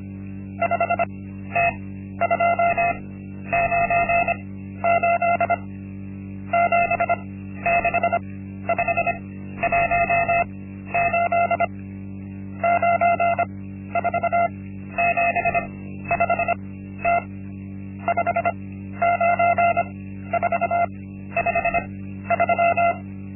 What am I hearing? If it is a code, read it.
Morse code, heard as a keyed tone, about 25 wpm, spaced as for 12 wpm, 690 Hz. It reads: 5T298 76518 9475T 59453
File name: Linksmakalnis-1